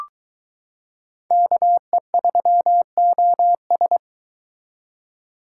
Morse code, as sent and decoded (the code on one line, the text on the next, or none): -.- . ...-- --- ...
KE3OS